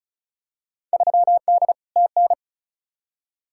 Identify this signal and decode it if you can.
Morse code, heard as a keyed tone, about 34 wpm, 700 Hz. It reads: 3D TN
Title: Field Day Exchanges Encore - 35wpm